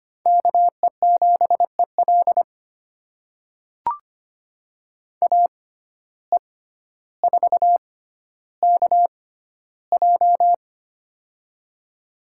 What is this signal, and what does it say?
Morse code, heard as a keyed tone, about 25 wpm, spaced as for 13 wpm, 700 Hz. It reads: KE7EL AE4KJ
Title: USA Call Signs - 5 Characters - 6x Character Spacing - 25wpm